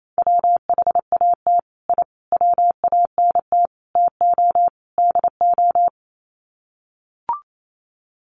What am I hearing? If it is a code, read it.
Morse code, heard as a keyed tone, about 28 wpm, 700 Hz. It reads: WHAT I WANT TO DO
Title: Sets of 5 Words Encore 28wpm